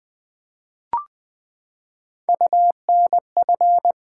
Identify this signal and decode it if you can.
Morse code, heard as a keyed tone, about 20 wpm, 700 Hz. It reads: UNF
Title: Build A Word - 3 Letters At A Time - 6-Letter Words - 20wpm